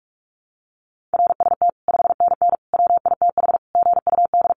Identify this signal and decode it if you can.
Morse code, heard as a keyed tone, about 44 wpm, 700 Hz. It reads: RST 5NN WITH GUD
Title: Common QSO Phrases 45wpm